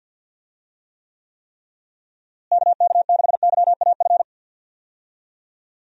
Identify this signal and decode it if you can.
Morse code, heard as a keyed tone, about 52 wpm, 700 Hz. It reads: KK6YAF
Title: USA Call Signs - 6 Characters - Repeated 3x - 50wpm